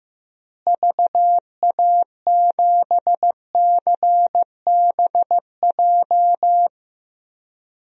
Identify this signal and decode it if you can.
Morse code, heard as a keyed tone, about 16 wpm, 700 Hz. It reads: VA7CBJ
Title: DXCC-100 Call Signs - 6 Characters - Repeated 3x - 15wpm